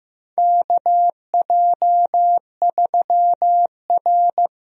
Morse code, heard as a keyed tone, about 15 wpm, 700 Hz. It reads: KJ3R
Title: USA Call Signs - 4 Characters - Repeated 3x - 15wpm